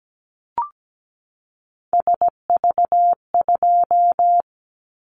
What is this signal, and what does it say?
Morse code, heard as a keyed tone, about 17 wpm, 700 Hz. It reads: SV2
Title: Callsign Prefixes for Top 50 DXCC - 17wpm